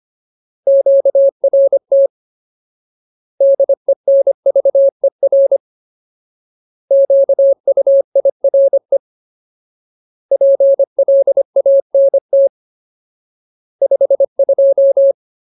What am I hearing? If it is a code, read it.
Morse code, heard as a keyed tone, about 25 wpm, 550 Hz. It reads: QRT DENVER QUIRE PLANT 52
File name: Ultimate Word List / 25WPM Ultimate